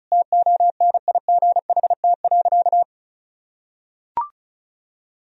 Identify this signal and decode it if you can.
Morse code, heard as a keyed tone, about 36 wpm, 700 Hz. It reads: TONIGHT.
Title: Sentences from Top 1000 Words - Review Part 4 - 35wpm